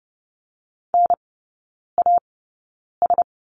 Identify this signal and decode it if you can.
Morse code, heard as a keyed tone, about 30 wpm, spaced as for 9 wpm, 700 Hz. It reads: NAS
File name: 081 Lesson 14e - Number 2 - Three Characters 30wpm